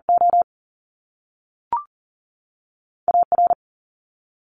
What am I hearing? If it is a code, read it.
Morse code, heard as a keyed tone, about 42 wpm, 700 Hz. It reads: O A R
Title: Top Two Letter Combinations in Words 40wpm